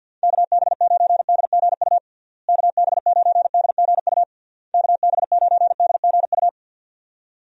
Morse code, heard as a keyed tone, about 46 wpm, 700 Hz. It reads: KB9DGU KB9DGU KB9DGU
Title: USA Call Signs - 6 Characters - Repeated 3x - 50wpm